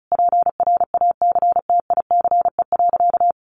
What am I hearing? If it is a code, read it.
Morse code, heard as a keyed tone, about 34 wpm, 700 Hz. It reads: PRACTICE.
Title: Additional Easy Phrases 35wpm